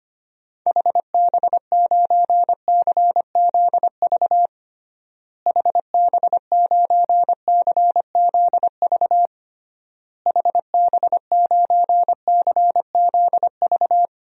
Morse code, heard as a keyed tone, about 25 wpm, 700 Hz. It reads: HB9CZV HB9CZV HB9CZV
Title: DXCC-100 Call Signs - 6 Characters - Repeated 3x - 25wpm